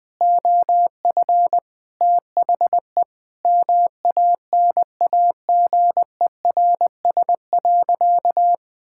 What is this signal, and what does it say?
Morse code, heard as a keyed tone, about 20 wpm, 700 Hz. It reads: OF THE MANAGERS.
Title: Sentences with 6-Word Phrases - Spoken Beginning - 20wpm